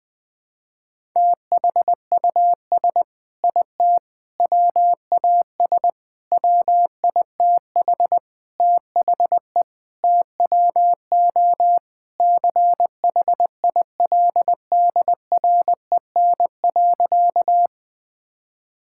Morse code, heard as a keyed tone, about 20 wpm, 700 Hz. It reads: THUS IT WAS WITH THE TWO CHILDREN.
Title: Sentences from Top 1000 Words - Review Part 5 - 20wpm